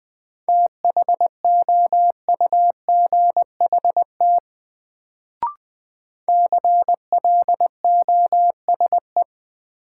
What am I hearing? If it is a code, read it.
Morse code, heard as a keyed tone, about 20 wpm, 700 Hz. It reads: THOUGHT CLOSE
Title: Sentences from Top 300 Words - New Word Drill - 20wpm